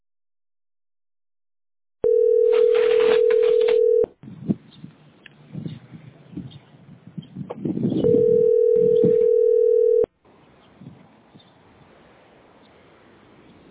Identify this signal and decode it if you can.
Ringback tone (North America), 440 + 480 Hz.